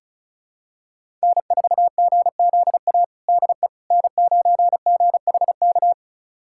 Morse code, heard as a keyed tone, about 36 wpm, 700 Hz. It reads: N4GZA DE N9GHK
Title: USA Callsign DE Callsign - 5 Characters - 35wpm